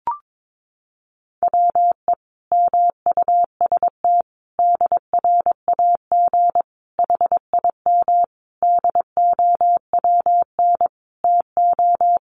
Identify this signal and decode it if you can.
Morse code, heard as a keyed tone, about 22 wpm, 700 Hz. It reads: WE MUST DRAG HIM DOWN TO
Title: Sentences from Top 1500 Words 22wpm